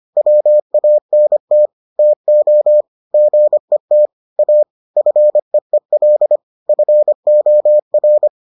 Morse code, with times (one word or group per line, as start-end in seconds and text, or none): 0.2-1.7 WANT
2.0-2.8 TO
3.1-4.1 GET
4.4-4.6 A
5.0-6.4 FEEL
6.7-8.3 FOR